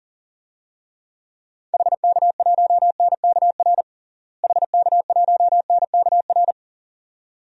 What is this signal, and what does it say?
Morse code, heard as a keyed tone, about 40 wpm, 700 Hz. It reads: HK1NKR HK1NKR
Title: DXCC-100 Call Signs - 6 Characters - Repeated 2x - 40wpm